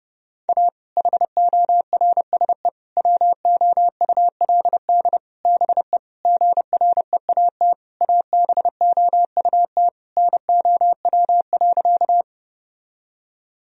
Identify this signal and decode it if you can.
Morse code, heard as a keyed tone, about 30 wpm, 700 Hz. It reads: A HORSE WOULD BE GREAT ABOUT NOW.
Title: Sentences from Top 1000 Words - Review Part 1 - 30wpm